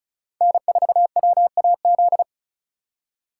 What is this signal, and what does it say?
Morse code, heard as a keyed tone, about 36 wpm, 700 Hz. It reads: N4WAZ
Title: USA Call Signs - 5 Characters - 3x Character Spacing - 35wpm